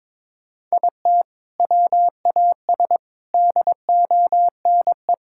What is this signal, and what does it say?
Morse code, heard as a keyed tone, about 22 wpm, 700 Hz. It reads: IT WAS DONE
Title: Sentences from Top 1000 Words - Review Part 5 - 22wpm